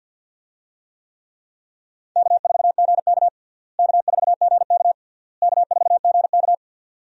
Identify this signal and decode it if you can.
Morse code, heard as a keyed tone, about 51 wpm, 700 Hz. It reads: K4GK K4GK K4GK
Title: USA Call Signs - 4 Characters - Repeated 3x - 50wpm